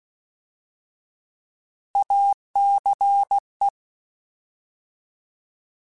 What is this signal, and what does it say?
Morse code, heard as a keyed tone, about 16 wpm, 800 Hz. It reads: ACE